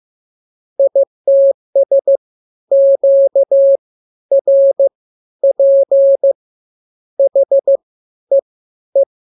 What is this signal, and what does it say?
Morse code, heard as a keyed tone, about 15 wpm, spaced as for 11 wpm, 550 Hz. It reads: ITS QRP HEE
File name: jokes / joke 4 15 0000